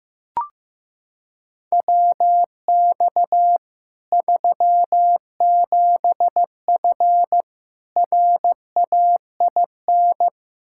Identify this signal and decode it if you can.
Morse code, heard as a keyed tone, about 15 wpm, 700 Hz. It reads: WX 37F RAIN
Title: Practice QSO - Exchange 2 - Info - 15wpm